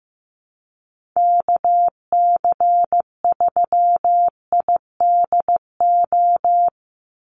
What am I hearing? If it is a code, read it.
Morse code, heard as a keyed tone, about 15 wpm, 700 Hz. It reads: KC3IDO
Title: USA Call Signs 15wpm